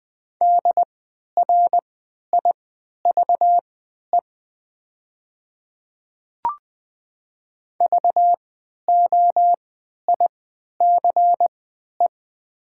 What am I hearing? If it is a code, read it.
Morse code, heard as a keyed tone, about 20 wpm, spaced as for 11 wpm, 700 Hz. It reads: DRIVE VOICE
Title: 3x Character Spacing - Top 500 Words - 20wpm